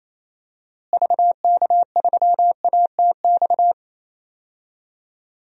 Morse code, heard as a keyed tone, about 28 wpm, 700 Hz. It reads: VK3ATX
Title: DXCC-100 Call Signs - 6 Characters - Repeated 3x - 28wpm